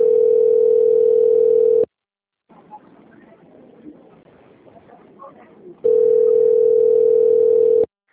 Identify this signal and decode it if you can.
Ringback tone (North America), 440 + 480 Hz, with speech.
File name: out- - - - - .46910